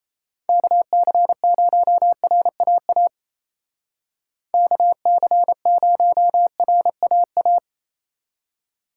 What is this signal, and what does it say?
Morse code, heard as a keyed tone, about 36 wpm, 700 Hz. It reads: KC0RAA KC0RAA